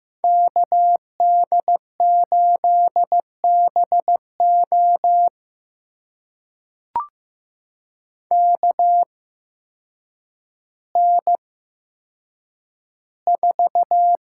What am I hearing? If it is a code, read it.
Morse code, heard as a keyed tone, about 15 wpm, spaced as for 7 wpm, 700 Hz. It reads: KD8BOK N4
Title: USA Call Signs - 8x Character Spacing - 15wpm